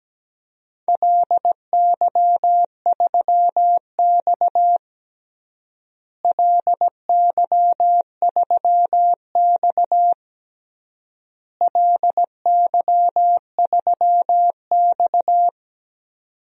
Morse code, heard as a keyed tone, about 17 wpm, 700 Hz. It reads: LY3X LY3X LY3X
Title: DXCC-100 Call Signs - 4 Characters - Repeated 3x - 17wpm